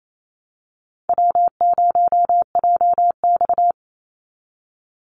Morse code, heard as a keyed tone, about 28 wpm, 700 Hz. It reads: W0JX
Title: Easy Call Signs Repeated 2x - 28wpm